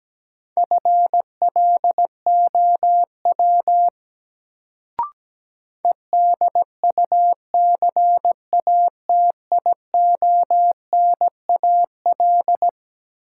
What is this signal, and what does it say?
Morse code, heard as a keyed tone, about 17 wpm, 700 Hz. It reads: FLOW EDUCATIONAL
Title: Sentences from Top 1200 Words - New Word Drill Alt - 17wpm